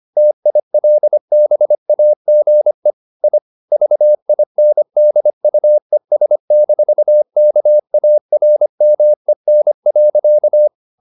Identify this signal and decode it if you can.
Morse code, heard as a keyed tone, about 25 wpm, 600 Hz. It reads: TILBAGE I VINDUES-KARMEN.
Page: Ferie 24 wpm